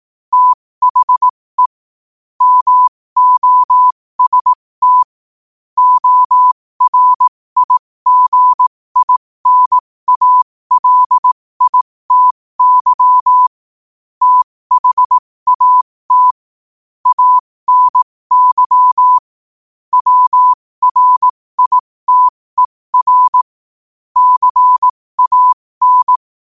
Morse code, heard as a keyed tone, about 18 wpm, spaced as for 15 wpm, 1000 Hz. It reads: THE MOST ORIGINALITY THAT ANY WRITER CAN
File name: qotdfiles / 15wpm